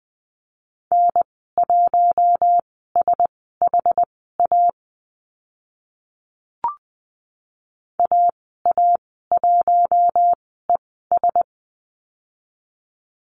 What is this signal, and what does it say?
Morse code, heard as a keyed tone, about 20 wpm, spaced as for 15 wpm, 700 Hz. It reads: N1SHA AA1ES
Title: 062 Lesson 11g - Letter h - Call Signs 20wpm